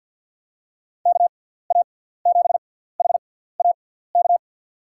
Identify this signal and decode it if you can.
Morse code, heard as a keyed tone, about 53 wpm, spaced as for 17 wpm, 700 Hz. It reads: KA7HAK
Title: USA Call Signs - 6 Characters - 6x Character Spacing - 50wpm